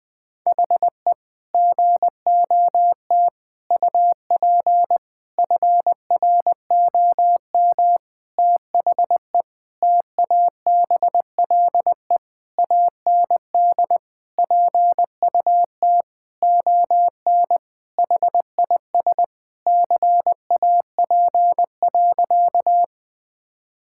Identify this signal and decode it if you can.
Morse code, heard as a keyed tone, about 21 wpm, 700 Hz. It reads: HE GOT UP FROM THE TABLE AND PUT ON HIS CAP.
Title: Sentences from Top 1600 Words 20wpm